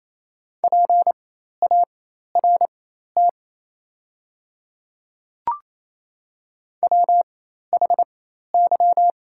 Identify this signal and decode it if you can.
Morse code, heard as a keyed tone, about 28 wpm, spaced as for 13 wpm, 700 Hz. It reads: PART WHY
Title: 4x Character Spacing - Top 200 Words - 28wpm